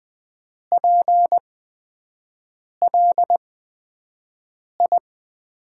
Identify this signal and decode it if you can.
Morse code, heard as a keyed tone, about 20 wpm, spaced as for 5 wpm, 700 Hz. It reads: PLI
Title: USA Call Signs - 5 Characters - 8x Character Spacing - 20wpm